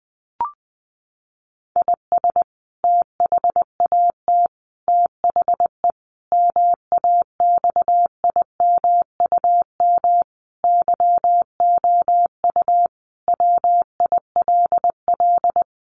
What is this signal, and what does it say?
Morse code, heard as a keyed tone, about 20 wpm, 700 Hz. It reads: IS THAT THE MAXIMUM YOU WILL
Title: Sentences from Top 1300 Words 20wpm